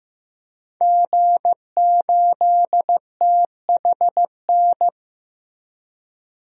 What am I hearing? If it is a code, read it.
Morse code, heard as a keyed tone, about 15 wpm, 700 Hz. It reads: G8THN
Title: DXCC-100 Call Signs - 5 Characters - Repeated 2x - 15wpm